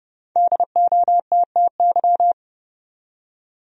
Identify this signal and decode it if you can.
Morse code, heard as a keyed tone, about 30 wpm, 700 Hz. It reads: DOTTY
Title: USA Common Names - Max 5 Letters - 30wpm